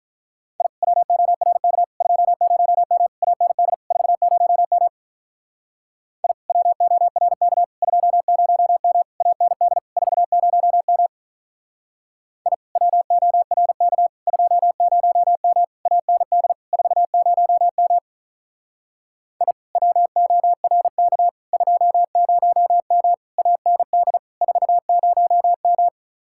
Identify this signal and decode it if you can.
Morse code, heard as a keyed tone, about 43 wpm, 700 Hz. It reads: I WORK 20M AND 40M I WORK 20M AND 40M I WORK 20M AND 40M I WORK 20M AND 40M
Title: Speed-Racing Common QSO Phrases - 35wpm